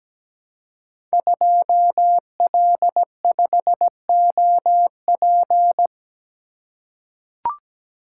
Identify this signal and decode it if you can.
Morse code, heard as a keyed tone, about 17 wpm, 700 Hz. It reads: 2L5OP
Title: Code Groups - 5 Letter:Numbers - 17wpm